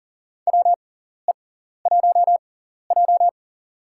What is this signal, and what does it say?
Morse code, heard as a keyed tone, about 37 wpm, spaced as for 14 wpm, 700 Hz. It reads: WE1J
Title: USA Call Signs - 4 Characters - 6x Character Spacing - 40wpm